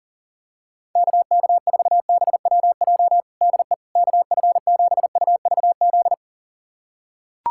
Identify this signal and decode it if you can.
Morse code, heard as a keyed tone, about 40 wpm, 700 Hz. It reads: KK4BWJ DE KF7UVZ
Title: USA Callsign DE Callsign - 6 Characters - 40wpm